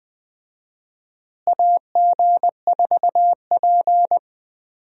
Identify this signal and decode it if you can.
Morse code, heard as a keyed tone, about 20 wpm, 700 Hz. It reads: AG4P
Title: USA Call Signs - 4 Characters - Repeated 3x - 20wpm